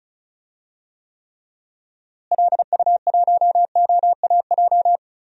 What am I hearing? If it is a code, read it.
Morse code, heard as a keyed tone, about 34 wpm, 700 Hz. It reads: LU1OAJ
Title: DXCC-100 Call Signs - 6 Characters - Repeated 2x - 35wpm